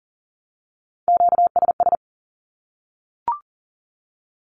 Morse code, heard as a keyed tone, about 40 wpm, 700 Hz. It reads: QSS
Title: Q-Codes 40wpm